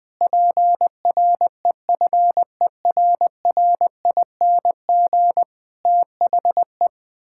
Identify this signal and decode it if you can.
Morse code, heard as a keyed tone, about 20 wpm, 700 Hz. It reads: PREFERRING THE
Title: Pangram Challenge v2 - 20wpm